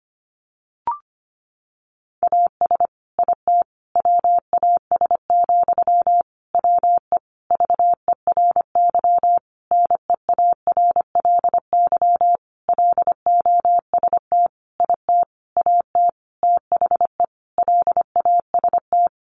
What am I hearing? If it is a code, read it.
Morse code, heard as a keyed tone, about 25 wpm, 700 Hz. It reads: AS IT WAS, WE VERY NEARLY LOST IT AT THE LAST
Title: Sentences from Top 1400 Words 25wpm